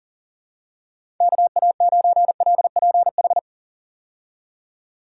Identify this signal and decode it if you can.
Morse code, heard as a keyed tone, about 40 wpm, 700 Hz. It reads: KA9LPH
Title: USA Call Signs - 6 Characters - 40wpm